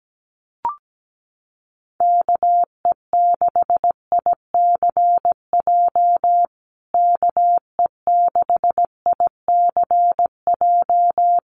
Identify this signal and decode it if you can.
Morse code, heard as a keyed tone, about 17 wpm, 700 Hz. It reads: KE6ICJ KE6ICJ
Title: USA Call Signs Repeated 2x 17wpm